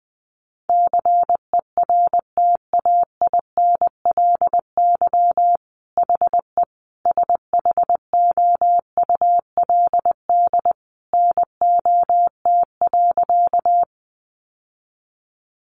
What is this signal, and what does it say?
Morse code, heard as a keyed tone, about 20 wpm, 700 Hz. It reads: CERTAINLY HE SHOULD NOT.
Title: Sentences from Top 900 Words 20wpm